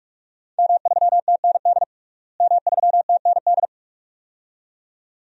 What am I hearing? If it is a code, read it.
Morse code, heard as a keyed tone, about 46 wpm, 700 Hz. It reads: M3TND M3TND
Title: DXCC-100 Call Signs - 5 Characters - Repeated 2x - 45wpm